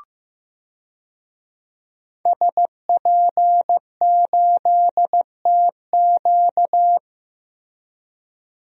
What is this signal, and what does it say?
Morse code, heard as a keyed tone, about 16 wpm, 700 Hz. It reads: SP8TQ
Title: DXCC-100 Call Signs - 5 Characters - Repeated 2x - 15wpm